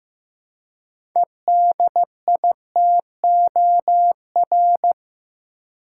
Morse code, heard as a keyed tone, about 16 wpm, 700 Hz. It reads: EDITOR
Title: Six Letter Words - 15wpm